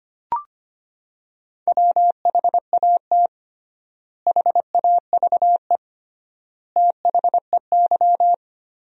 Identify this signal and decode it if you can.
Morse code, heard as a keyed tone, about 25 wpm, 700 Hz. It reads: WHAT HAVE THEY
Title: 3x Word-Spacing - Sentences from Top 500 Words - 25wpm